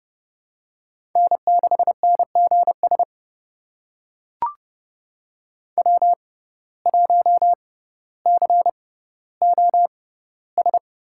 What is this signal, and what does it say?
Morse code, heard as a keyed tone, about 30 wpm, spaced as for 15 wpm, 700 Hz. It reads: N6NGS W1COS
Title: USA Call Signs - 5 Characters - 6x Character Spacing - 30wpm